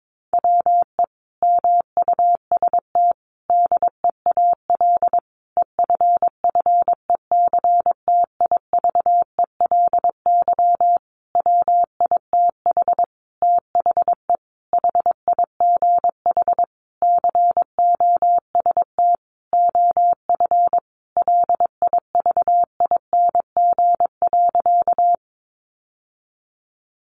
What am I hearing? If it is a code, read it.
Morse code, heard as a keyed tone, about 22 wpm, 700 Hz. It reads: WE MUST DEAL EFFECTIVELY WITH THE HIGH COST OF LIVING.
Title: Sentences from Top 1400 Words 22wpm